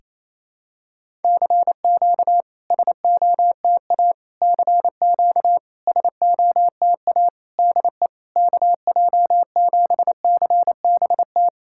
Morse code, heard as a keyed tone, about 28 wpm, 700 Hz. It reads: CQ SOTA CQ SOTA DE KJ7CBT
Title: SOTA QSO Practice 28wpm